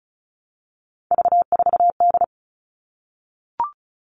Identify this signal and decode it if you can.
Morse code, heard as a keyed tone, about 34 wpm, 700 Hz. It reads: V4D
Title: All Cut Numbers - 3-Digit 35wpm